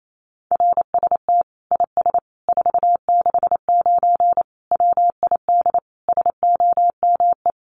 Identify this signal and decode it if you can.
Morse code, heard as a keyed tone, about 28 wpm, 700 Hz. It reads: RST IS 469 WID SOME
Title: Common QSO Phrases 28wpm